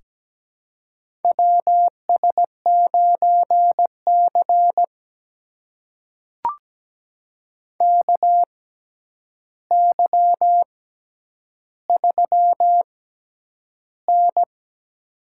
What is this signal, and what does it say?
Morse code, heard as a keyed tone, about 17 wpm, spaced as for 9 wpm, 700 Hz. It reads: WS9C KY3N
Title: USA Call Signs - 4 Characters - 6x Character Spacing - 17wpm